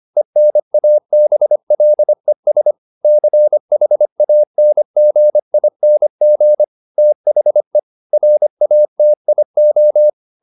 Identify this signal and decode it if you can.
Morse code, heard as a keyed tone, about 25 wpm, 600 Hz. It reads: ENABLES CHANGING THE RATIO